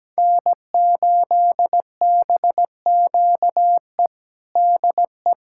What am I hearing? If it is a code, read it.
Morse code, heard as a keyed tone, about 17 wpm, 700 Hz. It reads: N8BQE DE
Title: USA Callsign DE Callsign 17wpm